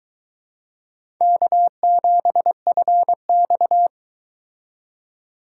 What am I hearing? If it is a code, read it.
Morse code, heard as a keyed tone, about 24 wpm, 700 Hz. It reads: K7FX